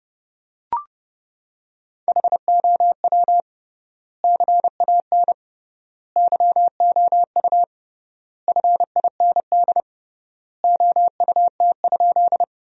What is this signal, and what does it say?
Morse code, heard as a keyed tone, about 30 wpm, 700 Hz. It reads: HOW CAN YOU FIND OUT?
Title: 3x Word-Spacing - Sentences from Top 200 Words - 30wpm